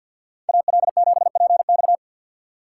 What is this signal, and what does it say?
Morse code, heard as a keyed tone, about 52 wpm, 700 Hz. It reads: AL7PX
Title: USA Call Signs - 3x Character Spacing - 50wpm